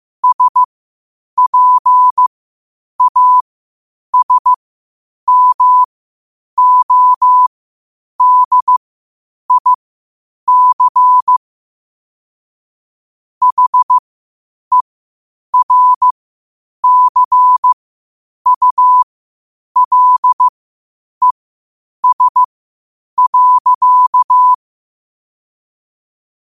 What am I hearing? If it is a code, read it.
Morse code, heard as a keyed tone, about 15 wpm, spaced as for 8 wpm, 1000 Hz. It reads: SPASMODIC HERCULES.
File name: qotdfiles / 8wpm